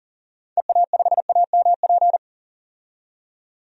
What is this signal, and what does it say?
Morse code, heard as a keyed tone, about 40 wpm, 700 Hz. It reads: EA5AMP
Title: DXCC-100 Call Signs - 6 Characters - 40wpm